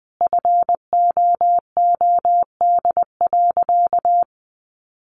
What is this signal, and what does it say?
Morse code, heard as a keyed tone, about 20 wpm, 700 Hz. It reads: FOOD.
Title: A Sound Of Thunder 20wpm